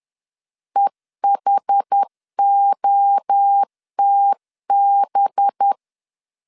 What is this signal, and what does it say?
Morse code, heard as a keyed tone, about 10 wpm, 800 Hz. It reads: EHOTB
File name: LEZIONI / 6 PKX0 / 6 PKX0 10wpm 2707